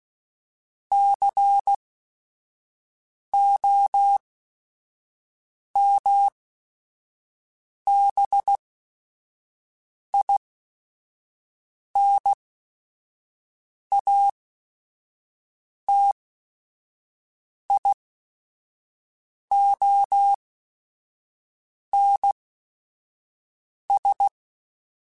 Morse code, heard as a keyed tone, about 16 wpm, spaced as for 5 wpm, 800 Hz. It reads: COMBINATIONS
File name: Technical Subjects / Technical Subjects-mp3s / AND Gate @5